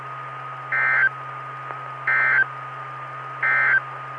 Emergency Alert System (SAME) tones, heard as harsh buzzing bursts. It is heard from 0.6 to 3.9 s.
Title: Saline, AK SKYWARN Repeater System - W5RHS